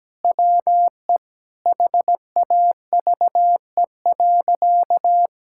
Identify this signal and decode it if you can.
Morse code, heard as a keyed tone, about 17 wpm, 700 Hz. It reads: WE HAVE.
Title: Sentences from Top 1000 Words - Review Part 6 - 17wpm